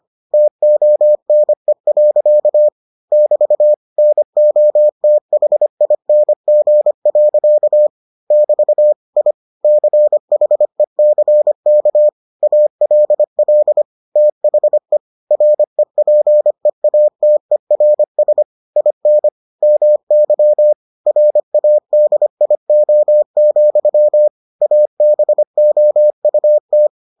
Morse code, heard as a keyed tone, about 25 wpm, 600 Hz. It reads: TONE. <BT> NOTHING. <BT> I CHECK ALL THE REPEATERS IN MY RADIO, ABOUT